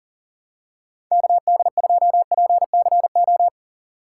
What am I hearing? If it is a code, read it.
Morse code, heard as a keyed tone, about 40 wpm, 700 Hz. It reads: KD2PCO
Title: USA Call Signs - 6 Characters - 3x Character Spacing - 40wpm